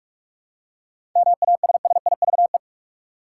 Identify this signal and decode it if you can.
Morse code, heard as a keyed tone, about 46 wpm, 700 Hz. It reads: MASSIVE
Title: Seven Letter Words - 45wpm